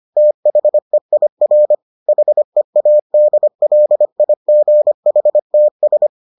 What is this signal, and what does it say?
Morse code, heard as a keyed tone, about 25 wpm, 600 Hz. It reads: THEIR HEADLIGHTS